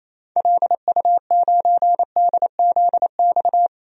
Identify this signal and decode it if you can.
Morse code, heard as a keyed tone, about 28 wpm, 700 Hz. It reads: LU9DZX
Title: DXCC-100 Call Signs - 6 Characters - 28wpm